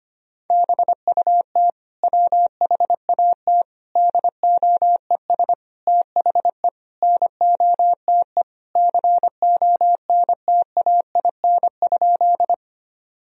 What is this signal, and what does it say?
Morse code, heard as a keyed tone, about 25 wpm, 700 Hz. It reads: BUT WHAT DOES THE NOTE CONTAIN?
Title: Sentences from Top 1000 Words - Review Part 2 - 25wpm